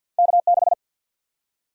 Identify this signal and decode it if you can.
Morse code, heard as a keyed tone, about 51 wpm, 700 Hz. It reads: K6
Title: USA Call Sign Prefixes - Repeated 2x - 50wpm